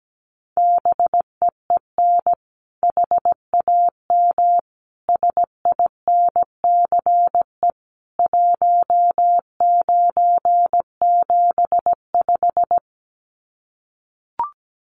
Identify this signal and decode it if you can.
Morse code, heard as a keyed tone, about 17 wpm, 700 Hz. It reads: BEEN HAM SINCE 1975
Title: Common QSO Phrases 17wpm